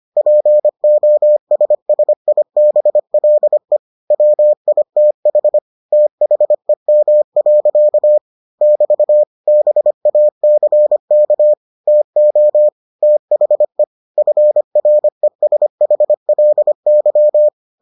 Morse code, heard as a keyed tone, about 25 wpm, 600 Hz. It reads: POSSIBLE WITH THEM. <BT> BACK TO THE FRESHLY